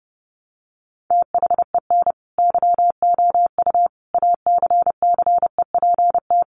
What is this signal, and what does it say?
Morse code, heard as a keyed tone, about 28 wpm, 700 Hz. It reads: THEN YOU ACCEPT
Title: Sentences from Top 1300 Words 30wpm